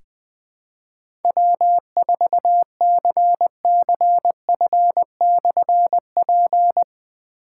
Morse code, heard as a keyed tone, about 20 wpm, 700 Hz. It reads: W4CCF/P
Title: Difficult Call Signs 20wpm